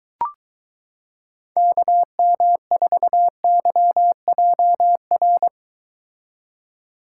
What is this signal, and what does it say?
Morse code, heard as a keyed tone, about 23 wpm, 700 Hz. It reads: KM4YJR